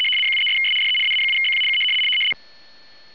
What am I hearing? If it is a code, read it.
RTTY, about 100 baud. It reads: 73'S DE AA7BQ K
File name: w ry850 / rtty100